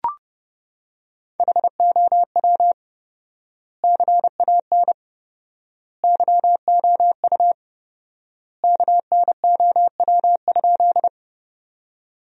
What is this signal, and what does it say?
Morse code, heard as a keyed tone, about 30 wpm, 700 Hz. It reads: HOW CAN YOU KNOW?
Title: 4x Word-Spacing - Sentences from Top 100 Words - 30wpm